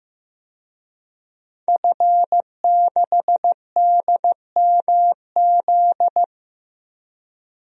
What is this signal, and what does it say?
Morse code, heard as a keyed tone, about 15 wpm, 700 Hz. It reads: F6DMZ
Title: DXCC-100 Call Signs - 5 Characters - Repeated 2x - 15wpm